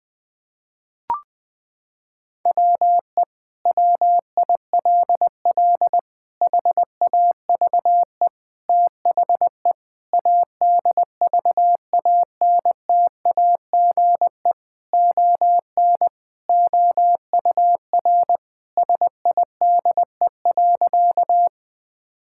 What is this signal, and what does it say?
Morse code, heard as a keyed tone, about 20 wpm, 700 Hz. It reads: WE WILL HAVE THE ADVANTAGE ON OUR SIDE.
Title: Sentences from Top 1000 Words - Review Part 6 - 20wpm